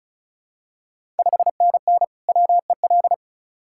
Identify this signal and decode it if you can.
Morse code, heard as a keyed tone, about 34 wpm, 700 Hz. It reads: 5NN WEL
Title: Ontario QSO Party Exchange 35wpm